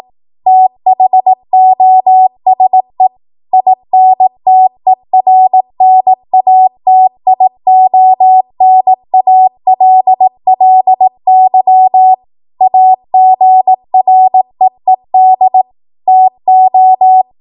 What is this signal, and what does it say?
Morse code, heard as a keyed tone, about 18 wpm, 750 Hz. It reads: THOSE INTERNATIONALLY AGREED TO
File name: Morse / 18WPM